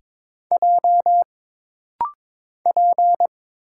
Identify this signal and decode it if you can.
Morse code, heard as a keyed tone, about 22 wpm, 700 Hz. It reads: J P
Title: Single Letter-Number - Rapid-Fire 22wpm